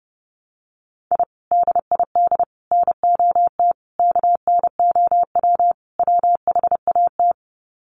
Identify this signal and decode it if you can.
Morse code, heard as a keyed tone, about 28 wpm, 700 Hz. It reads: I DID NOT KNOW WHAT
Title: Sets of 5 Words 30wpm